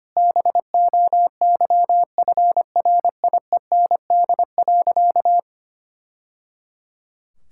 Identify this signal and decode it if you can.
Morse code, heard as a keyed tone, about 25 wpm, 700 Hz. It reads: BOYFRIEND.
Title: Sentences from Top 1000 Words - Review Part 6 - 25wpm